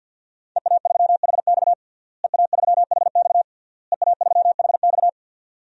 Morse code, heard as a keyed tone, about 51 wpm, 700 Hz. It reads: EA3HX EA3HX EA3HX
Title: DXCC-100 Call Signs - 5 Characters - Repeated 3x - 50wpm